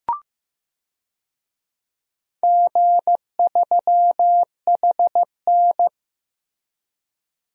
Morse code, heard as a keyed tone, about 15 wpm, 700 Hz. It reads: G3HN
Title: DXCC-100 Call Signs - 4 Characters - Repeated 3x - 15wpm